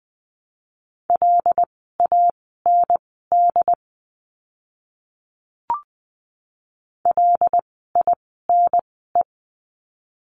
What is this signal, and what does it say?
Morse code, heard as a keyed tone, about 20 wpm, spaced as for 14 wpm, 700 Hz. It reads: LAND LINE
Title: 075 Lesson 13f - Letter L - Words 20wpm